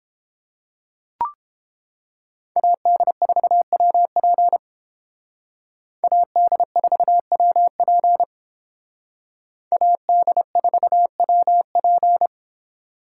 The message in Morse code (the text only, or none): AD4WP AD4WP AD4WP